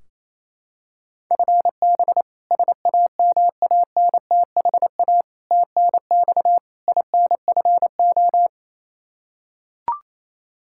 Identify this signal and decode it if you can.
Morse code, heard as a keyed tone, about 28 wpm, 700 Hz. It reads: FB SAMANTHA TNX INFO
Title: K1USN SST QSO Practice 28wpm